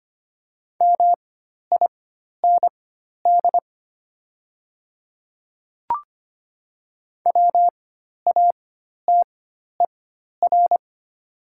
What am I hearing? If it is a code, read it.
Morse code, heard as a keyed tone, about 25 wpm, spaced as for 12 wpm, 690 Hz. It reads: MIND WATER
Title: 4x Character Spacing - Top 500 Words - 25wpm